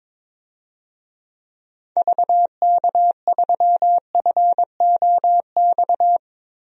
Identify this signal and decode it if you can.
Morse code, heard as a keyed tone, about 22 wpm, 700 Hz. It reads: VK3FOX
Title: DXCC-100 Call Signs - 6 Characters - Repeated 3x - 22wpm